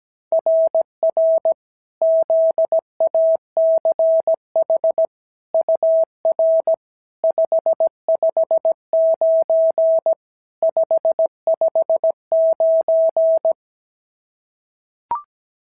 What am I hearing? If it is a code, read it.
Morse code, heard as a keyed tone, about 17 wpm, 650 Hz. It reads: RR ZACH UR 559 559
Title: SOTA QSO Practice 17wpm